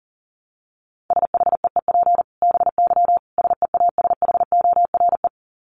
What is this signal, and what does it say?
Morse code, heard as a keyed tone, about 40 wpm, 700 Hz. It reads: SHEEP BY SEASHORE
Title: Dit Challenge 40wpm